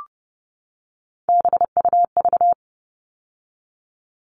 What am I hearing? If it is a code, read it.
Morse code, heard as a keyed tone, about 30 wpm, 700 Hz. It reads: BUV
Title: All Cut Numbers - 3-Digit 30wpm